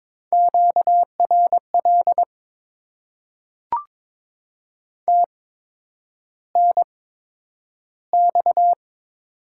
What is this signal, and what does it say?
Morse code, heard as a keyed tone, about 23 wpm, spaced as for 9 wpm, 700 Hz. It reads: QRL TNX
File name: Top 20 QSO Elements - 8x Character Spacing - 22wpm